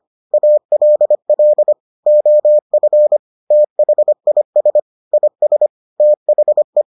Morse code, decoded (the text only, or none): ALL OF THIS IS THE